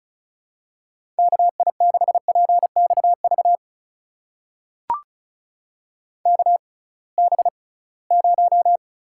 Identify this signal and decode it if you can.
Morse code, heard as a keyed tone, about 36 wpm, spaced as for 23 wpm, 700 Hz. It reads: KI6PXV KB0
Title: USA Call Signs - 6 Characters - 6x Character Spacing - 35wpm